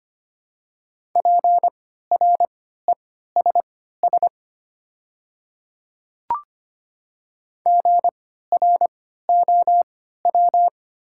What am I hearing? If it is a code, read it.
Morse code, heard as a keyed tone, about 25 wpm, spaced as for 14 wpm, 700 Hz. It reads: PRESS GROW
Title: 3x Character Spacing - Top 300 Words - 25wpm